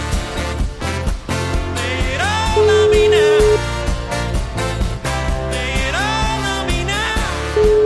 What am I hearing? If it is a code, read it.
Ringback tone (Europe, 425 Hz), with noise.